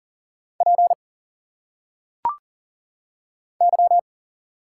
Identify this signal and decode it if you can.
Morse code, heard as a keyed tone, about 40 wpm, 700 Hz. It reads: P Y
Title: Single Letters - Phonetic Pronunciation 40wpm